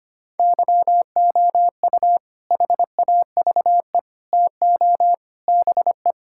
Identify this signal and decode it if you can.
Morse code, heard as a keyed tone, about 25 wpm, 700 Hz. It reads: YOU HAVE TO BE
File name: practice / 25wpm / Sentences with 4-Word Phrases - 25wpm